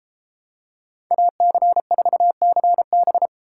Morse code, heard as a keyed tone, about 32 wpm, 700 Hz. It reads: AC4CB